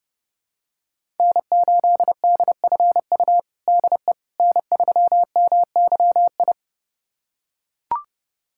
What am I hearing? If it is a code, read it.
Morse code, heard as a keyed tone, about 32 wpm, 700 Hz. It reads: N8DFU DE N3MYI
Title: USA Callsign DE Callsign - 5 Characters - 30wpm